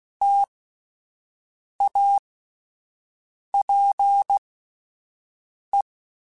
Morse code, heard as a keyed tone, about 16 wpm, spaced as for 5 wpm, 800 Hz. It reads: TAPE